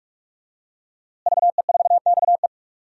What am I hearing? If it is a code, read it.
Morse code, heard as a keyed tone, about 43 wpm, 700 Hz. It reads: VE4XE
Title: DXCC-100 Call Signs - 5 Characters - Repeated 2x - 45wpm